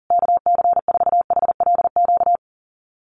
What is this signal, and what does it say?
Morse code, heard as a keyed tone, about 40 wpm, 700 Hz. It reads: KC4HLQ
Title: USA Call Signs Repeated 3x 40wpm